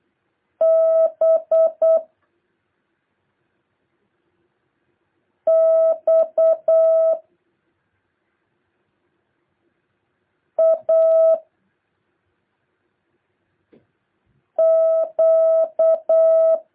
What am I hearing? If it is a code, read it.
Morse code, heard as a keyed tone, about 8 wpm, 650 Hz. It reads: B X A Q
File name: jh8cbh / 2023cw